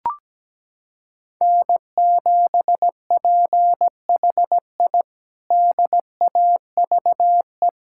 Morse code, heard as a keyed tone, about 17 wpm, 700 Hz. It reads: N7PHI DAVE
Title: K1USN SST QSO Practice - 17wpm